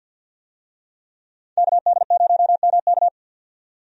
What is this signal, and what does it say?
Morse code, heard as a keyed tone, about 50 wpm, 700 Hz. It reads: KD0MK
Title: USA Call Signs - 5 Characters - 50wpm